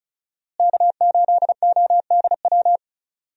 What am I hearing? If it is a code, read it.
Morse code, heard as a keyed tone, about 36 wpm, 700 Hz. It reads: K8ODW
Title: USA Call Signs - 5 Characters - Repeated 3x - 35wpm